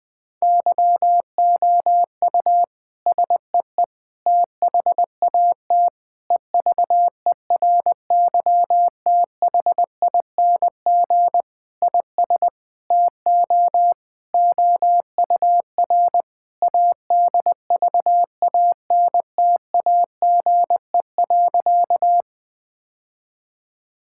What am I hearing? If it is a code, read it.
Morse code, heard as a keyed tone, about 20 wpm, 700 Hz. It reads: YOU SEE THAT EVERYTHING IS TO OUR ADVANTAGE.
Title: Sentences from Top 1000 Words - Review Part 6 - 20wpm